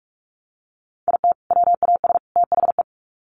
Morse code, heard as a keyed tone, about 44 wpm, 700 Hz. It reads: IT WAS THE
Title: Sets of 3 Words 45wpm